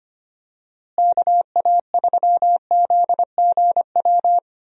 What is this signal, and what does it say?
Morse code, heard as a keyed tone, about 25 wpm, 700 Hz. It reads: KA3ZGW
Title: USA Call Signs - 6 Characters - Repeated 3x - 25wpm